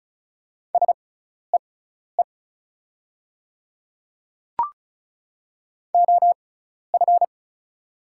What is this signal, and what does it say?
Morse code, heard as a keyed tone, about 34 wpm, spaced as for 12 wpm, 690 Hz. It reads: SEE OF
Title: 6x Character Spacing - Top 100 Words - 35wpm